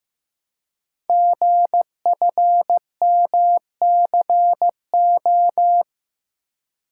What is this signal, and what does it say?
Morse code, heard as a keyed tone, about 15 wpm, 700 Hz. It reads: GFMCO
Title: Code Groups - 5 Letters - 15wpm